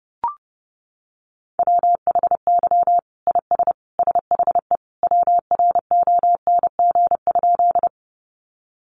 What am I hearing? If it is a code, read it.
Morse code, heard as a keyed tone, about 30 wpm, 700 Hz. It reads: WHY IS SHE WRONG?
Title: Sentences from Top 900 Words 30wpm